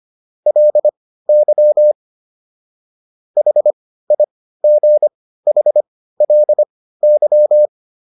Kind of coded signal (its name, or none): Morse code